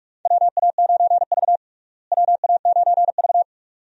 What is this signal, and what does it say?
Morse code, heard as a keyed tone, about 42 wpm, 700 Hz. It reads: WA9V WA9V
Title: USA Call Signs - 4 Characters - Repeated 2x - 45wpm